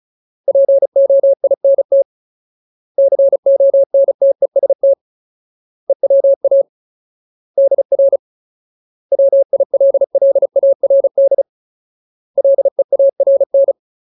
Morse code, heard as a keyed tone, about 34 wpm, 550 Hz. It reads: POINT CONTEST EWA DR WILLARD LEARN
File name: Ultimate Word List / 35WPM Ultimate